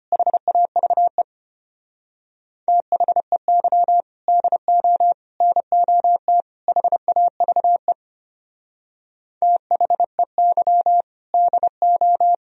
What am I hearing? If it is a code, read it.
Morse code, heard as a keyed tone, about 32 wpm, 700 Hz. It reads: HAVE THEY DO NOT HAVE THEY DO